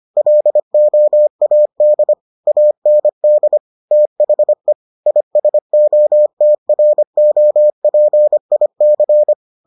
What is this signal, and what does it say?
Morse code, heard as a keyed tone, about 25 wpm, 600 Hz. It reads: LOAD AND THE ISOTROPIC